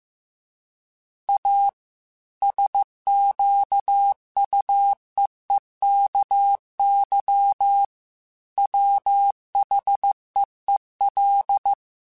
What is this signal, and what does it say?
Morse code, heard as a keyed tone, about 15 wpm, 800 Hz. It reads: A SQUEEKY WHEEL
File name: Chuckles / Chuckles-mp3s / Squeaky @13